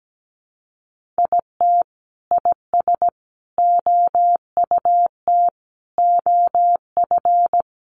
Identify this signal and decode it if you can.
Morse code, heard as a keyed tone, about 17 wpm, 700 Hz. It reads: IT IS OUT OF
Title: Sentences from Top 1400 Words 17wpm